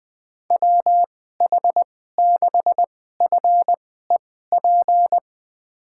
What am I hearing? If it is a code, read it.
Morse code, heard as a keyed tone, about 21 wpm, spaced as for 15 wpm, 700 Hz. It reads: WH6FEP
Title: USA Call Signs - 6 Characters - 2x Character Spacing - 20wpm